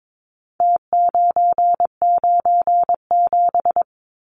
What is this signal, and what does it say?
Morse code, heard as a keyed tone, about 22 wpm, 700 Hz. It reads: T997
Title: Numbers - Contest Style - 4-Digit 22wpm